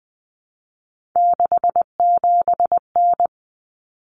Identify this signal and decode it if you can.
Morse code, heard as a keyed tone, about 20 wpm, 700 Hz. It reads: 67N
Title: Common Cut Numbers - 3-Digit 20wpm